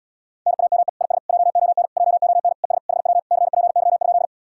Morse code, heard as a keyed tone, about 80 wpm, 700 Hz. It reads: NAME IS JON JON ES HW CPY?